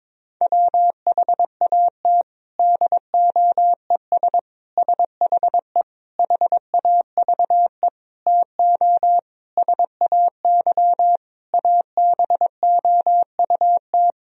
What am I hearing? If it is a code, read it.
Morse code, heard as a keyed tone, about 22 wpm, 700 Hz. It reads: WHAT DOES SHE HAVE TO SAY ABOUT
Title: Sentences from Top 1000 Words - Review Part 3 - 22wpm